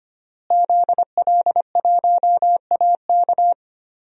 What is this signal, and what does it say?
Morse code, heard as a keyed tone, about 25 wpm, 700 Hz. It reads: ZL1AK
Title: DXCC-100 Call Signs - 5 Characters - Repeated 3x - 25wpm